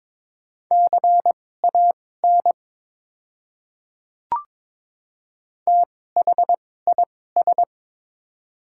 Morse code, heard as a keyed tone, about 22 wpm, spaced as for 16 wpm, 700 Hz. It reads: CAN THIS
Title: 2x Character Spacing - Top 100 Words - 22wpm